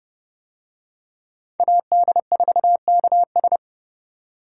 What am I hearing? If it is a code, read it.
Morse code, heard as a keyed tone, about 32 wpm, 700 Hz. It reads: AD4KS